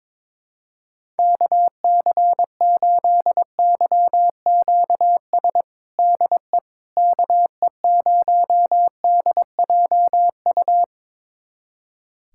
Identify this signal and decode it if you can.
Morse code, heard as a keyed tone, about 22 wpm, 700 Hz. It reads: KC8YQS DE KE0DJU
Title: USA Callsign DE Callsign 22wpm